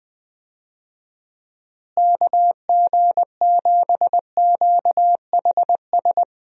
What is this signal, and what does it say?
Morse code, heard as a keyed tone, about 20 wpm, 700 Hz. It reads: KG7QHS
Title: USA Call Signs - 6 Characters - Repeated 2x - 20wpm